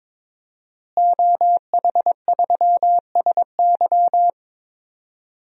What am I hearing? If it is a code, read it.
Morse code, heard as a keyed tone, about 22 wpm, 700 Hz. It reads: OH3SY
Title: DXCC-100 Call Signs - 5 Characters - Repeated 2x - 22wpm